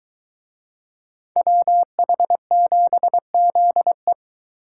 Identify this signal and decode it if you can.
Morse code, heard as a keyed tone, about 23 wpm, 700 Hz. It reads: WH7ZE